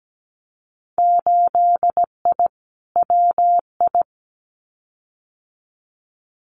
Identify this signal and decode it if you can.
Morse code, heard as a keyed tone, about 17 wpm, 700 Hz. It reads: 8I WI
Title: Winter Field Day Exchanges 17wpm